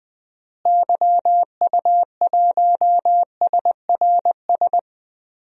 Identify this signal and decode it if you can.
Morse code, heard as a keyed tone, about 20 wpm, 700 Hz. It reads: YU1SRS
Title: DXCC-100 Call Signs - 6 Characters - Repeated 2x - 20wpm